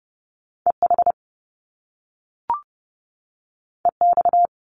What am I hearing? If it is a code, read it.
Morse code, heard as a keyed tone, about 30 wpm, 700 Hz. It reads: EH EX
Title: Top Two Letter Combinations in Words 30wpm